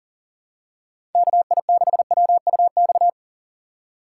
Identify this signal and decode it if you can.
Morse code, heard as a keyed tone, about 40 wpm, 700 Hz. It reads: KI6WUX
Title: USA Call Signs - 4x Character Spacing - 40wpm